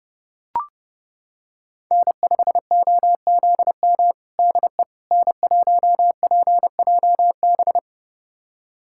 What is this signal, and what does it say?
Morse code, heard as a keyed tone, about 30 wpm, 700 Hz. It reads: N5OZM DE N1PJB
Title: USA Callsign DE Callsign - 5 Characters - 30wpm